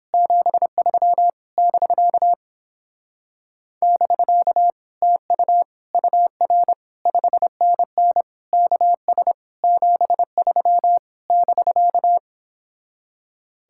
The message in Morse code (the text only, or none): 73 <BK> <BK> TU UR 5NN KS 73 <BK>